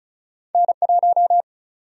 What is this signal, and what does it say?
Morse code, heard as a keyed tone, about 36 wpm, 700 Hz. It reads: N1
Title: USA Call Sign Prefixes - Repeated 2x - 35wpm